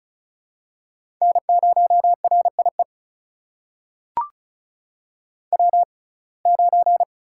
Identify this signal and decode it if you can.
Morse code, heard as a keyed tone, about 36 wpm, spaced as for 25 wpm, 700 Hz. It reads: N0RIE W9
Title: USA Call Signs - 5 Characters - 6x Character Spacing - 35wpm